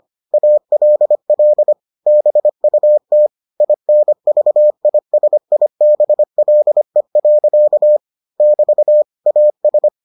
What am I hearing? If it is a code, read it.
Morse code, heard as a keyed tone, about 25 wpm, 600 Hz. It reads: ALL BUT INVISIBLE. <BT> AS